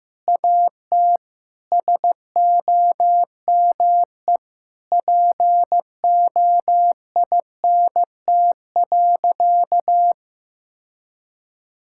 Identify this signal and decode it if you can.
Morse code, heard as a keyed tone, about 15 wpm, 700 Hz. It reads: AT SOME POINT.
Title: Sentences from Top 1000 Words - Review Part 5 - 15wpm